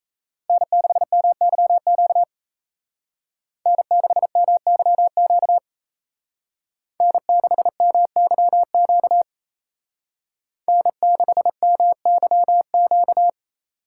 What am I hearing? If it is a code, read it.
Morse code, heard as a keyed tone, about 36 wpm, 700 Hz. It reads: N6MYQ N6MYQ N6MYQ N6MYQ